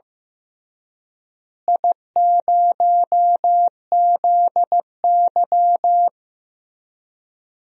Morse code, heard as a keyed tone, about 15 wpm, 700 Hz. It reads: I0ZY
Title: DXCC-100 Call Signs - 4 Characters - Repeated 2x - 15wpm